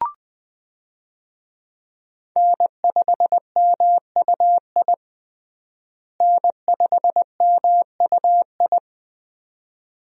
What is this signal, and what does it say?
Morse code, heard as a keyed tone, about 21 wpm, 700 Hz. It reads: N5MUI N5MUI
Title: USA Call Signs - 5 Characters - Repeated 2x - 20wpm